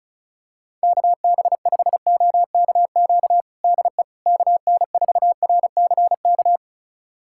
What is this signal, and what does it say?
Morse code, heard as a keyed tone, about 36 wpm, 700 Hz. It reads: KB5OKQ DE KN4RCK
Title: USA Callsign DE Callsign 35wpm